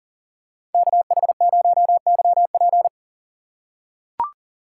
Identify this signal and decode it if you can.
Morse code, heard as a keyed tone, about 40 wpm, 700 Hz. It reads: KH0YP
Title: USA Call Signs - 4x Character Spacing - 40wpm